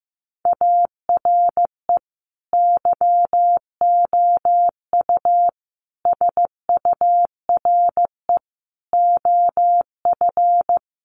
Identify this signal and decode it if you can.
Morse code, heard as a keyed tone, about 15 wpm, 700 Hz. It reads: ARE YOU SURE OF
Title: Sentences from Top 300 Words 15wpm